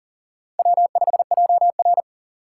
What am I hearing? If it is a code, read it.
Morse code, heard as a keyed tone, about 40 wpm, 700 Hz. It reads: W5JR
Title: USA Call Signs - 4 Characters - 6x Character Spacing - 40wpm